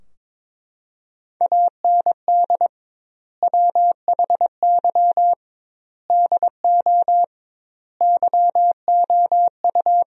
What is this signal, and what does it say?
Morse code, heard as a keyed tone, about 22 wpm, 700 Hz. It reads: AND WHY DO YOU
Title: 2x Word-Spacing - Sentences from Top 400 Words - 22wpm